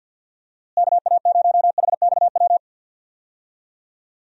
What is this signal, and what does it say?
Morse code, heard as a keyed tone, about 46 wpm, 700 Hz. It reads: KA0HXW
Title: USA Call Signs - 6 Characters - Repeated 2x - 50wpm